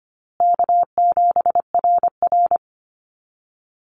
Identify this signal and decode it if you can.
Morse code, heard as a keyed tone, about 25 wpm, 700 Hz. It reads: K7RR
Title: Call Signs Repeated 2x - 25wpm